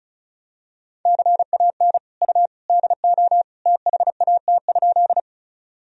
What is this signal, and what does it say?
Morse code, heard as a keyed tone, about 36 wpm, 700 Hz. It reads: CAN U DO THAT?
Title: Sentences from Top 100 Words and Abbreviations 35wpm